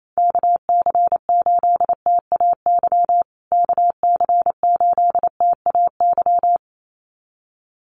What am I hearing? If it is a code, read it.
Morse code, heard as a keyed tone, about 28 wpm, 700 Hz. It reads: KC8TAY KC8TAY
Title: USA Call Signs Repeated 2x 28wpm